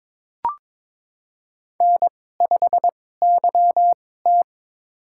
Morse code, heard as a keyed tone, about 22 wpm, spaced as for 16 wpm, 700 Hz. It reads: N5YT
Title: USA Call Signs - 4 Characters - 2x Character Spacing - 22wpm